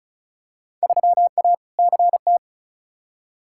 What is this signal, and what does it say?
Morse code, heard as a keyed tone, about 34 wpm, 700 Hz. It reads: 3A CT
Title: Field Day Exchanges - 35wpm